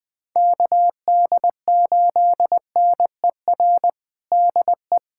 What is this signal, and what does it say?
Morse code, heard as a keyed tone, about 20 wpm, 700 Hz. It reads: KD8NER DE
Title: USA Callsign DE Callsign - 6 Characters - 20wpm